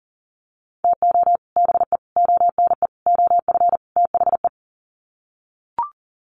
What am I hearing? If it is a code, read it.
Morse code, heard as a keyed tone, about 40 wpm, 700 Hz. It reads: TO BE ONE OF THE
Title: Sets of 5 Words 40wpm